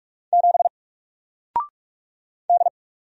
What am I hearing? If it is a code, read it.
Morse code, heard as a keyed tone, about 44 wpm, 700 Hz. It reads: 7 D
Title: Single Character (Letter:Number:Punctuation:Prosign) 45wpm